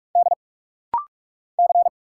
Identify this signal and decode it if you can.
Morse code, heard as a keyed tone, about 44 wpm, 700 Hz. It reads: D C
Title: Single Letters - Rapid-Fire 45wpm